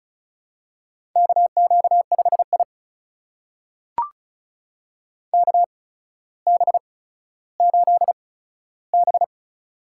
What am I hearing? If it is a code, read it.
Morse code, heard as a keyed tone, about 36 wpm, spaced as for 15 wpm, 700 Hz. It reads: KQ5I KB8B
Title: USA Call Signs - 8x Character Spacing - 35wpm